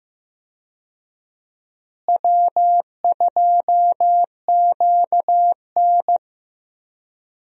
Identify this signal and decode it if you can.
Morse code, heard as a keyed tone, about 15 wpm, 700 Hz. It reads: W2QN
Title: USA Call Signs - 4 Characters - Repeated 3x - 15wpm